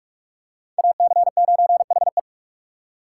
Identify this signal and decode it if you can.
Morse code, heard as a keyed tone, about 44 wpm, 700 Hz. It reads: AC9HE